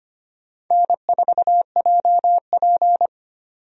Morse code, heard as a keyed tone, about 25 wpm, 700 Hz. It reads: N4JP
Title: USA Call Signs - 4 Characters - 25wpm